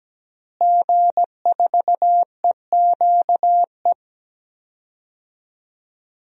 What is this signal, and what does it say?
Morse code, heard as a keyed tone, about 17 wpm, 700 Hz. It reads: G4EQE
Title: DXCC-100 Call Signs - 5 Characters - Repeated 2x - 17wpm